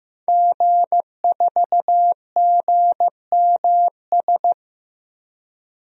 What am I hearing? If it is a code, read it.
Morse code, heard as a keyed tone, about 15 wpm, 700 Hz. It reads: G4GMS
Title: DXCC-100 Call Signs - 5 Characters - Repeated 3x - 15wpm